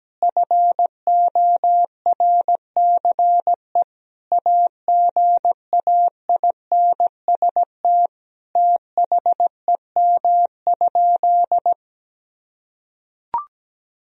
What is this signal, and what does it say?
Morse code, heard as a keyed tone, about 17 wpm, 700 Hz. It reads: FORCE AGAINST THEM?
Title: Sentences from Top 1000 Words - Review Part 4 - 17wpm